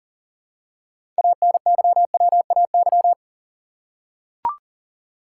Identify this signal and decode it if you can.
Morse code, heard as a keyed tone, about 40 wpm, 700 Hz. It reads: ANYWAY
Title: Six Letter Words - 40wpm